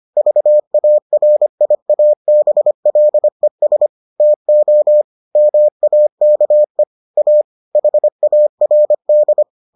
Morse code, heard as a keyed tone, about 25 wpm, 600 Hz. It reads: VARIABLES TO MAKE A HARD